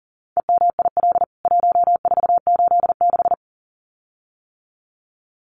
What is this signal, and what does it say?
Morse code, heard as a keyed tone, about 40 wpm, 700 Hz. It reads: EMIL 1486
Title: CWT Exchange 40wpm